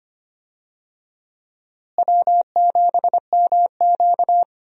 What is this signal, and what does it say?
Morse code, heard as a keyed tone, about 25 wpm, 700 Hz. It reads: W7MQ
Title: USA Call Signs - 4 Characters - Repeated 3x - 25wpm